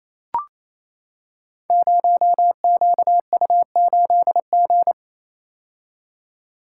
Morse code, heard as a keyed tone, about 29 wpm, 700 Hz. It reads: 0QU8G
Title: Code Groups - 5 Letter:Numbers - 28wpm